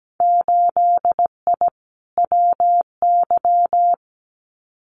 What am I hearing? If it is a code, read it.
Morse code, heard as a keyed tone, about 17 wpm, 700 Hz. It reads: 8I WY
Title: Winter Field Day Exchanges Encore 17wpm